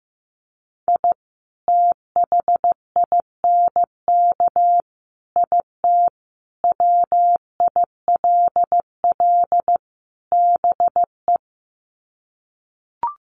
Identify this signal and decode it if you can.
Morse code, heard as a keyed tone, about 15 wpm, 700 Hz. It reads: I THINK IT WILL BE
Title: Sets of 5 Words Encore2 15wpm